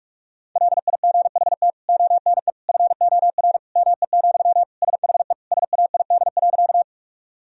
Morse code, heard as a keyed tone, about 47 wpm, 700 Hz. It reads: LIGHT ONE FOR ME, SHE SAID.
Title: Sentences from Top 1000 Words - Review Part 1 - 45wpm